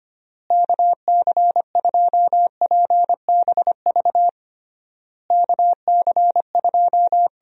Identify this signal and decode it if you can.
Morse code, heard as a keyed tone, about 25 wpm, 700 Hz. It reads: KC2PBV KC2
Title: USA Call Signs - 6 Characters - Repeated 3x - 25wpm